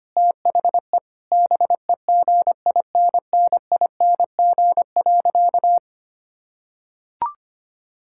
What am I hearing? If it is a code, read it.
Morse code, heard as a keyed tone, about 25 wpm, 700 Hz. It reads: THE BEGINNING.
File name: practice / 25wpm / Sentences with 4-Word Phrases - 25wpm